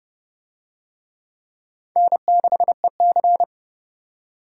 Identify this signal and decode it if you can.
Morse code, heard as a keyed tone, about 27 wpm, 700 Hz. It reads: N6EC